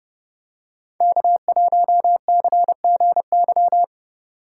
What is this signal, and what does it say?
Morse code, heard as a keyed tone, about 28 wpm, 700 Hz. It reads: K1CGY